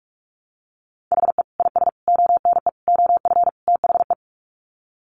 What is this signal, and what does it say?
Morse code, heard as a keyed tone, about 44 wpm, 700 Hz. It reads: HE IS ONE OF THE
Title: Sets of 5 Words Encore 45wpm